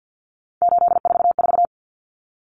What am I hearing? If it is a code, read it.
Morse code, heard as a keyed tone, about 50 wpm, 700 Hz. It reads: 844
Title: Common Cut Numbers - 3-Digit 50wpm